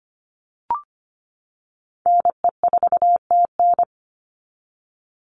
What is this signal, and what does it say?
Morse code, heard as a keyed tone, about 25 wpm, 700 Hz. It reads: NE4TN
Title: NA SOTA Chaser Callsigns 25wpm